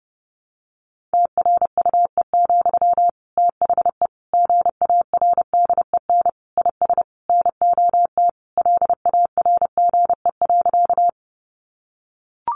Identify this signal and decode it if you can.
Morse code, heard as a keyed tone, about 30 wpm, 700 Hz. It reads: TRUE, THE GARDEN IS NOT LARGE.
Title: Sentences from Top 700 Words 30wpm